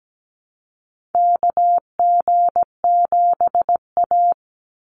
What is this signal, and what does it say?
Morse code, heard as a keyed tone, about 17 wpm, 700 Hz. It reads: KG7A
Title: Call Signs - 17wpm